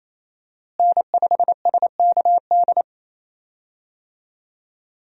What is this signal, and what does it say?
Morse code, heard as a keyed tone, about 28 wpm, 700 Hz. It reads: N5SKD
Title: USA Call Signs - 5 Characters - Repeated 2x - 28wpm